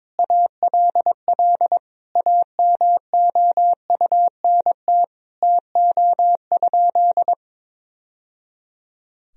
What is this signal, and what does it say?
Morse code, heard as a keyed tone, about 22 wpm, 700 Hz. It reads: ALL AMOUNT TO?
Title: Sentences from Top 1000 Words - Review Part 4 - 22wpm